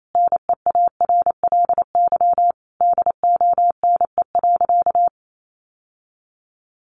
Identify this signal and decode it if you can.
Morse code, heard as a keyed tone, about 28 wpm, 700 Hz. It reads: NEARLY DONE.
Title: Sentences from Top 1300 Words 28wpm